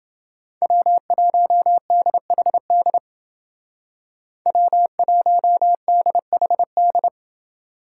Morse code, heard as a keyed tone, about 29 wpm, 700 Hz. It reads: W1DHD W1DHD